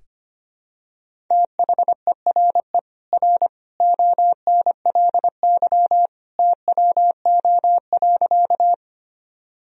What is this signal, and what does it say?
Morse code, heard as a keyed tone, about 25 wpm, 700 Hz. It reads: THERE R ONLY TWO.
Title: Sentences from Top 200 Words and Abbreviations 25wpm